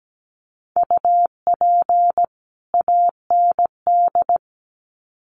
Morse code, heard as a keyed tone, about 17 wpm, 700 Hz. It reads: UP AND
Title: Sets of 2 Words 17wpm